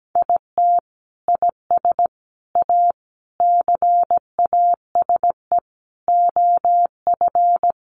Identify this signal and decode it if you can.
Morse code, heard as a keyed tone, about 17 wpm, 700 Hz. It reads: IT IS A CASE OF
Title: Sentences from Top 1400 Words 17wpm